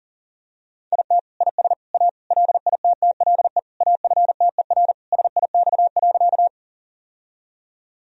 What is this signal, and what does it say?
Morse code, heard as a keyed tone, about 40 wpm, 700 Hz. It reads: IT IS A LITTLE AFTER SIX.
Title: Sentences from Top 1000 Words - Review Part 2 - 40wpm